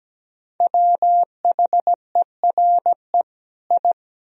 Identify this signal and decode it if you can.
Morse code, heard as a keyed tone, about 17 wpm, 700 Hz. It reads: WHERE I
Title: Sentences from Top 1000 Words - Review Part 1 - 17wpm